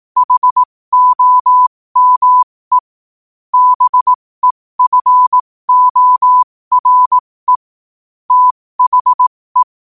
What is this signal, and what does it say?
Morse code, heard as a keyed tone, about 18 wpm, spaced as for 15 wpm, 1000 Hz. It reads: HOME BEFORE THE
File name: qotdfiles / 15wpm